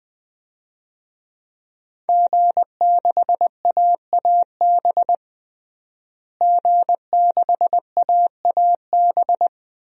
Morse code, heard as a keyed tone, about 20 wpm, 700 Hz. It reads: G6AAB G6AAB
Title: DXCC-100 Call Signs - 5 Characters - Repeated 2x - 20wpm